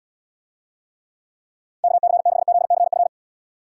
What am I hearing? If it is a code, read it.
Morse code, heard as a keyed tone, about 70 wpm, 700 Hz. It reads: KC6ZCF